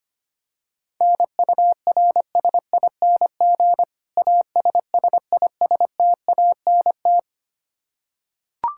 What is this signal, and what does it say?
Morse code, heard as a keyed tone, about 25 wpm, 700 Hz. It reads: NURSING ASSISTANT
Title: Practice QSO - Exchange 3 - More Info - 25wpm